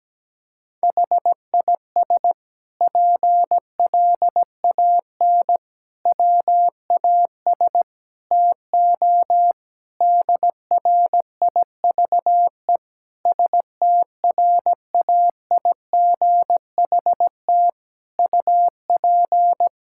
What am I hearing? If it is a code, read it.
Morse code, heard as a keyed tone, about 17 wpm, 700 Hz. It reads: HIS PLAN WAS TO DRIVE STRAIGHT UP
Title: Sentences from Top 1000 Words - Review Part 6 - 17wpm